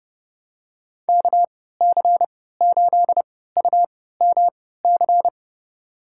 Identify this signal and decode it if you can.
Morse code, heard as a keyed tone, about 32 wpm, spaced as for 17 wpm, 700 Hz. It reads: KC8UMC
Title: USA Call Signs - 6 Characters - 3x Character Spacing - 30wpm